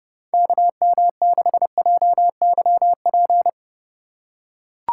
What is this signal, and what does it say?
Morse code, heard as a keyed tone, about 31 wpm, 700 Hz. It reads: KM6JYP
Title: USA Call Signs - 4x Character Spacing - 30wpm